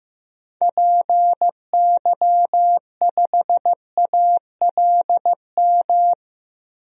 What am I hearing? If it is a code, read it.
Morse code, heard as a keyed tone, about 15 wpm, 700 Hz. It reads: PY5ALM
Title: DXCC-100 Call Signs - 6 Characters - Repeated 3x - 15wpm